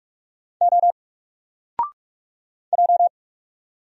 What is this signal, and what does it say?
Morse code, heard as a keyed tone, about 45 wpm, 700 Hz. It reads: O J
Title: Single Letter-Number 45wpm